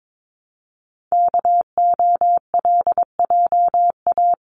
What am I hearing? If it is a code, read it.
Morse code, heard as a keyed tone, about 22 wpm, 700 Hz. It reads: KOLJA
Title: CWT Exchange 22wpm